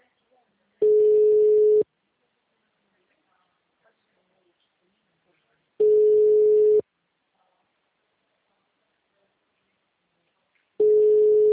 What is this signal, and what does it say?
Ringback tone (Europe, 425 Hz), with speech.
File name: out- -2010- - - .6